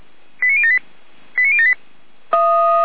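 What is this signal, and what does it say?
Five-tone selective call (ZVEI), heard as quick runs of five tones, tones of about 70 ms. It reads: 89078 89078
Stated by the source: ZVEI five-tone alert received by SDR, from the audio archive of the feuerwehr-orlamuende fire brigade.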